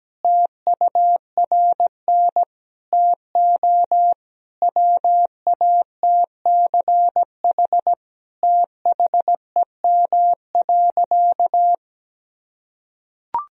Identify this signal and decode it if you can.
Morse code, heard as a keyed tone, about 17 wpm, 700 Hz. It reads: TURN TO WATCH THEM.
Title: Sentences from Top 1000 Words - Review Part 1 - 17wpm